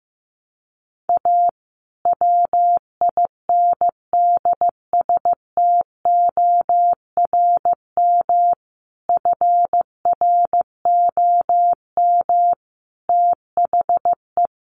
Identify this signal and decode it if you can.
Morse code, heard as a keyed tone, about 15 wpm, 700 Hz. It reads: A WINDSTORM FROM THE
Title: A Sound Of Thunder 15wpm